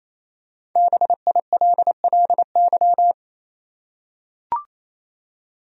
Morse code, heard as a keyed tone, about 28 wpm, 700 Hz. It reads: BILLY
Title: USA Common Names - Max 5 Letters - 28wpm